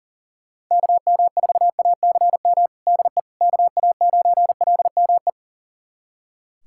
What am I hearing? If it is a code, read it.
Morse code, heard as a keyed tone, about 40 wpm, 700 Hz. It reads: KM4ACM DE KA9LME
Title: USA Callsign DE Callsign 40wpm